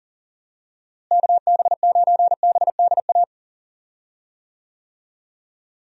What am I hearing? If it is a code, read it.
Morse code, heard as a keyed tone, about 37 wpm, 700 Hz. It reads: KB9BDA
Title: USA Call Signs - 6 Characters - Repeated 2x - 40wpm